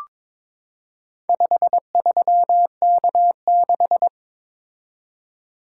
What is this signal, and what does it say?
Morse code, heard as a keyed tone, about 22 wpm, 690 Hz. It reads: 53K6
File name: practice / 22wpm / Four Letter-Number Combinations - 22wpm